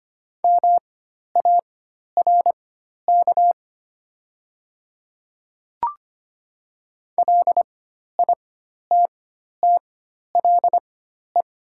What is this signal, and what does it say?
Morse code, heard as a keyed tone, about 25 wpm, spaced as for 12 wpm, 700 Hz. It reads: MARK LITTLE
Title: 4x Character Spacing - Top 300 Words - 25wpm